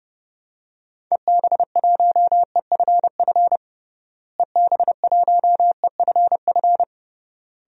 Morse code, heard as a keyed tone, about 30 wpm, 700 Hz. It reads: EB1EFF EB1EFF
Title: DXCC-100 Call Signs - 6 Characters - Repeated 3x - 30wpm